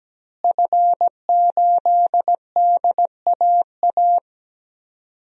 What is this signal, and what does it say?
Morse code, heard as a keyed tone, about 17 wpm, 700 Hz. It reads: F8DAA
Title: DXCC-100 Call Signs - 5 Characters - Repeated 3x - 17wpm